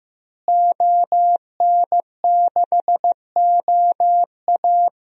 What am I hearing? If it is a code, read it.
Morse code, heard as a keyed tone, about 16 wpm, 700 Hz. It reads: ON6OA
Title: DXCC-100 Call Signs - 5 Characters - Repeated 3x - 15wpm